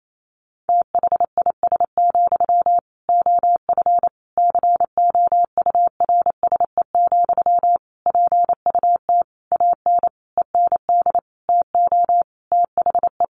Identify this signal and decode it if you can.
Morse code, heard as a keyed tone, about 28 wpm, 700 Hz. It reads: THIS, OF COURSE, PUT AN END TO THE
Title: Sentences from Top 1400 Words 28wpm